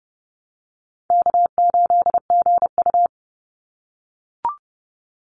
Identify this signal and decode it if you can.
Morse code, heard as a keyed tone, about 30 wpm, 700 Hz. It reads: K8GU
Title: Easy Call Signs - 30wpm